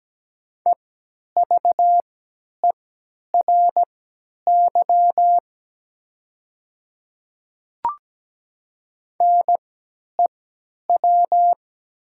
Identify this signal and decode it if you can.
Morse code, heard as a keyed tone, about 17 wpm, spaced as for 10 wpm, 700 Hz. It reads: EVERY NEW
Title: 3x Character Spacing - Top 200 Words - 17wpm